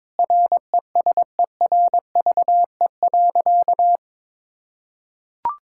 Morse code, heard as a keyed tone, about 22 wpm, 700 Hz. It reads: RESERVE.
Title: Sentences from Top 1000 Words - Review Part 6 - 22wpm